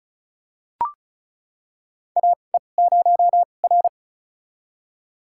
Morse code, heard as a keyed tone, about 36 wpm, spaced as for 25 wpm, 700 Hz. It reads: AE0R
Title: USA Call Signs - 4 Characters - 2x Character Spacing - 35wpm